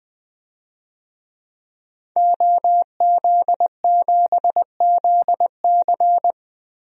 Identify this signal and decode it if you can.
Morse code, heard as a keyed tone, about 20 wpm, 700 Hz. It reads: OZ7ZC
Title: DXCC-100 Call Signs - 5 Characters - Repeated 2x - 20wpm